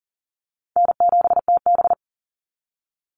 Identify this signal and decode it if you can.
Morse code, heard as a keyed tone, about 40 wpm, 700 Hz. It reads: N7TB
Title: Easy Call Signs Repeated 2x - 40wpm